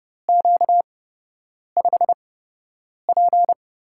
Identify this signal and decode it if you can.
Morse code, heard as a keyed tone, about 29 wpm, spaced as for 8 wpm, 700 Hz. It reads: Q5P
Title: USA Call Signs - 4 Characters - 8x Character Spacing - 30wpm